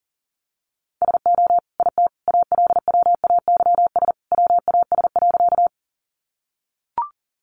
Morse code, heard as a keyed tone, about 40 wpm, 700 Hz. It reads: SO IT ALWAYS WAS.
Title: Sentences from Top 300 Words 40wpm